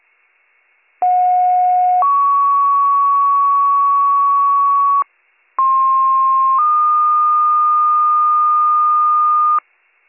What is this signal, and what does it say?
Two-tone page, heard as a one-second tone, then a three-second tone, about 730 then 1120 Hz, about 1060 then 1220 Hz. It is heard from 1.0 to 9.6 s.